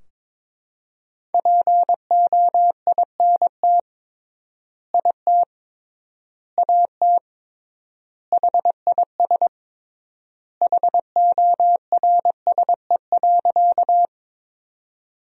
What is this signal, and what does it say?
Morse code, heard as a keyed tone, about 21 wpm, 700 Hz. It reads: POINT IT AT HIS HORSE.
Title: 3x Word-Spacing - Sentences from Top 300 Words - 22wpm